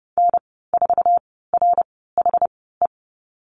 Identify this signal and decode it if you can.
Morse code, heard as a keyed tone, about 30 wpm, spaced as for 17 wpm, 700 Hz. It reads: N4RHE
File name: Lesson 14g - Number 2 - Call Signs 30wpm